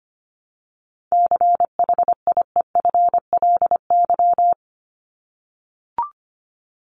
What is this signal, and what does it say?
Morse code, heard as a keyed tone, about 25 wpm, 700 Hz. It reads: CHIEFLY
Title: LY Suffix - Top 100 Words with Base Words 25wpm